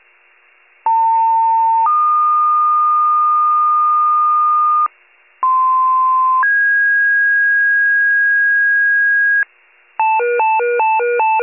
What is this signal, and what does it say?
Two-tone page, heard as a one-second tone, then a three-second tone, about 900 then 1220 Hz, about 1040 then 1650 Hz, with noise. It is heard from 0.9 to 9.4 s.
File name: wcalert